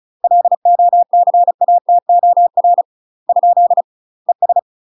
Morse code, heard as a keyed tone, about 35 wpm, 700 Hz. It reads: LOCATOR ? ES